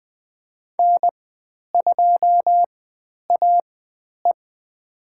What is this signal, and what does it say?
Morse code, heard as a keyed tone, about 20 wpm, spaced as for 10 wpm, 700 Hz. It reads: N2AE
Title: 109 Lesson 18g - Letter m - Call Signs 20wpm